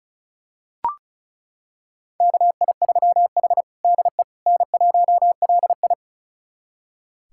Morse code, heard as a keyed tone, about 36 wpm, 700 Hz. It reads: KI3H DE N1LI
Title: USA Callsign DE Callsign - 4 Characters - 35wpm